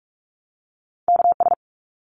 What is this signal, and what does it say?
Morse code, heard as a keyed tone, about 44 wpm, 700 Hz. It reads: KS
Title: ARRL Sections 45wpm